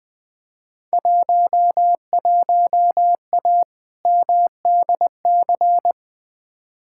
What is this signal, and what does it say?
Morse code, heard as a keyed tone, about 20 wpm, 700 Hz. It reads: 11A MDC
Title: Field Day Exchanges Encore - 20wpm